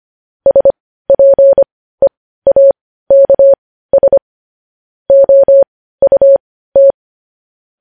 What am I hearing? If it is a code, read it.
Morse code, heard as a keyed tone, about 25 wpm, spaced as for 15 wpm, 550 Hz. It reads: SPEAKS OUT
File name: podcast / News headlines-1x-25x15